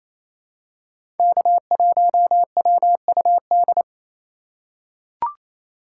Morse code, heard as a keyed tone, about 28 wpm, 700 Hz. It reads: K1WUD
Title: USA Call Signs - 5 Characters - 2x Character Spacing - 28wpm